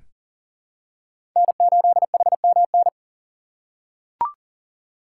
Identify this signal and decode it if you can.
Morse code, heard as a keyed tone, about 40 wpm, 700 Hz. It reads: N8HMN
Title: USA Call Signs - 5 Characters - 2x Character Spacing - 40wpm